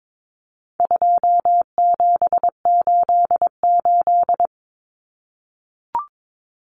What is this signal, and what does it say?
Morse code, heard as a keyed tone, about 22 wpm, 700 Hz. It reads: 2788
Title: Common Cut Numbers - 4-Digit 22wpm